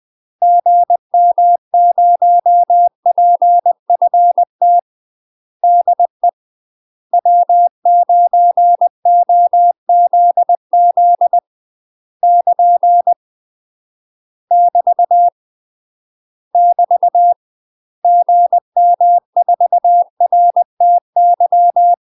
Morse code, heard as a keyed tone, about 20 wpm, 700 Hz. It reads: GM0PFT DE W9OZZ <KN> <BT> <BT> GM4RTY
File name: UK MASTER QSOs TRAINING / 20WPM / UK MASTER QSOs 0006